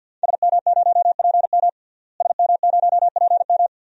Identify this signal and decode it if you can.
Morse code, heard as a keyed tone, about 51 wpm, 700 Hz. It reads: SM0PM SM0PM
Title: DXCC-100 Call Signs - 5 Characters - Repeated 2x - 50wpm